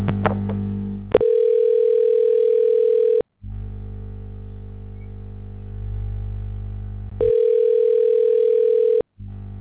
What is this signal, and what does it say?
Ringback tone (North America), 440 + 480 Hz.